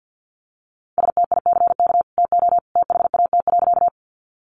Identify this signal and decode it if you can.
Morse code, heard as a keyed tone, about 50 wpm, 700 Hz. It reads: STICK TO THAT.
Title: Sentences from Top 1200 Words 50wpm